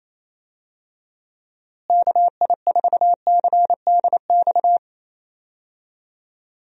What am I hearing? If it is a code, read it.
Morse code, heard as a keyed tone, about 27 wpm, 700 Hz. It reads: KI4CDX
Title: USA Call Signs - 6 Characters - Repeated 2x - 28wpm